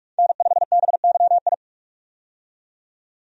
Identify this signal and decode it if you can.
Morse code, heard as a keyed tone, about 46 wpm, 700 Hz. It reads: N5BYI
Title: USA Call Signs - 5 Characters - 45wpm